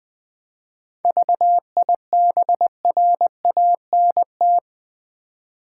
Morse code, heard as a keyed tone, about 20 wpm, 700 Hz. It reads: VIBRANT
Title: 7 Letter Words - First 3 Letters Given - 20wpm